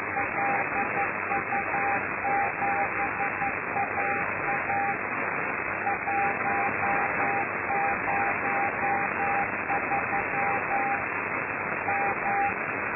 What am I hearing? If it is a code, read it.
Morse code, heard as a keyed tone, about 11 wpm, 800 Hz, with noise. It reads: LU7AA 103 M
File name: lu4aao / Globo Amsat